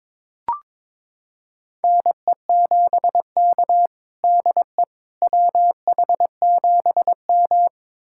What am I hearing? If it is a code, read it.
Morse code, heard as a keyed tone, about 22 wpm, 700 Hz. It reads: NE7K DE WH7M
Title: USA Callsign DE Callsign - 4 Characters - 22wpm